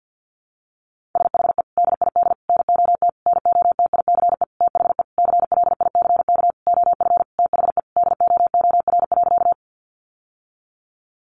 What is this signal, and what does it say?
Morse code, heard as a keyed tone, about 50 wpm, 700 Hz. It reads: SHE DID NOT NOTICE THE CLICK OF THE DOOR.
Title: Sentences from Top 1100 Words 50wpm